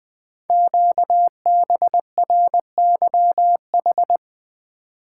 Morse code, heard as a keyed tone, about 20 wpm, 700 Hz. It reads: QBRYH
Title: Code Groups - 5 Letters - 20wpm